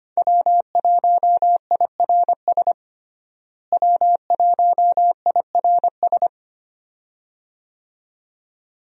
Morse code, heard as a keyed tone, about 25 wpm, 700 Hz. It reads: W1IRS W1IRS
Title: USA Call Signs - 5 Characters - Repeated 2x - 25wpm